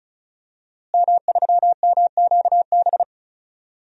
Morse code, heard as a keyed tone, about 36 wpm, 700 Hz. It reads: M3MQB
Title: DXCC-100 Call Signs - 5 Characters - Repeated 3x - 35wpm